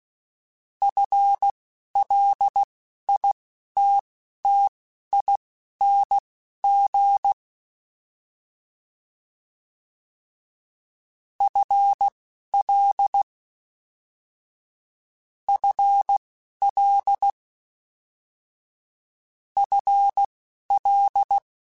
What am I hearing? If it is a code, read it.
Morse code, heard as a keyed tone, about 16 wpm, spaced as for 11 wpm, 800 Hz. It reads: FLITTING FL FL FL
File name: FL / FL mp3s / FL Prefixes 3x2 @10